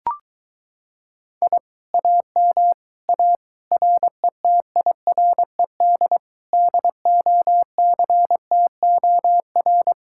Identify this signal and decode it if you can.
Morse code, heard as a keyed tone, about 23 wpm, 700 Hz. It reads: I AM A RETIRED DOCTOR